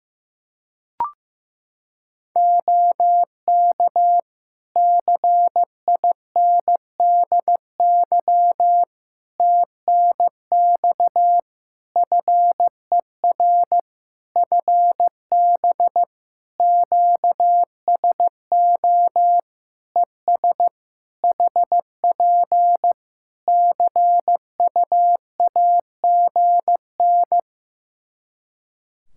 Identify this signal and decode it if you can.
Morse code, heard as a keyed tone, about 16 wpm, 700 Hz. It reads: OK CINDY TNX FER FB QSO ES HP CUAGN
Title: K1USN SST QSO Practice 15wpm